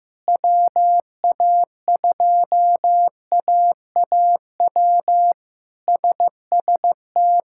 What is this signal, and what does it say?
Morse code, heard as a keyed tone, about 16 wpm, 700 Hz. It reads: WA2AAW SST
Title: K1USN SST QSO Practice - 15wpm